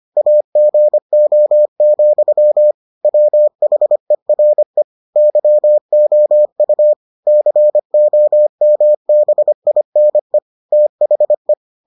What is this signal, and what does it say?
Morse code, heard as a keyed tone, about 25 wpm, 600 Hz. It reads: AGO, WHERE YOU COMBINE THE